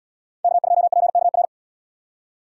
Morse code, heard as a keyed tone, about 77 wpm, 700 Hz. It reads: K2PKR